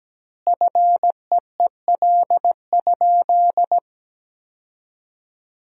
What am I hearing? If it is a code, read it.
Morse code, heard as a keyed tone, about 17 wpm, 700 Hz. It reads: FEEL?
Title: 2x Word-Spacing - Sentences from Top 400 Words - 17wpm